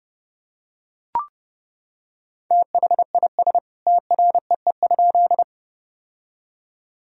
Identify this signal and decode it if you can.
Morse code, heard as a keyed tone, about 30 wpm, 700 Hz. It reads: THIS TREE?
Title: Sentences from Top 1000 Words - Review Part 1 - 30wpm